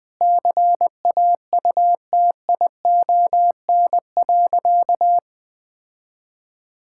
Morse code, heard as a keyed tone, about 21 wpm, 700 Hz. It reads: CAUTION.
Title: Sentences with 7-Word Phrases - Spoken Beginning - 20wpm